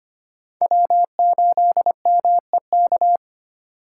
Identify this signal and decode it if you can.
Morse code, heard as a keyed tone, about 33 wpm, 700 Hz. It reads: W8MEK